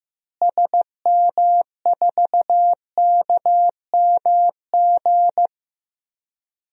Morse code, heard as a keyed tone, about 15 wpm, 700 Hz. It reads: SM4KMG
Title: DXCC-100 Call Signs - 6 Characters - Repeated 3x - 15wpm